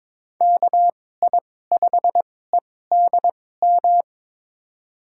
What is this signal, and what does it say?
Morse code, heard as a keyed tone, about 22 wpm, spaced as for 16 wpm, 700 Hz. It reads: KI5EDM
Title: USA Call Signs - 2x Character Spacing - 22wpm